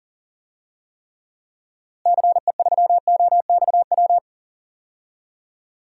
Morse code, heard as a keyed tone, about 40 wpm, 700 Hz. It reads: CE3OXW
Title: DXCC-100 Call Signs - 6 Characters - Repeated 3x - 40wpm